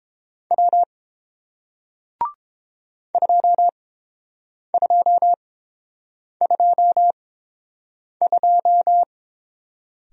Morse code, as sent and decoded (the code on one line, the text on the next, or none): .-- / ..--- ..--- ..--- ..---
W 2222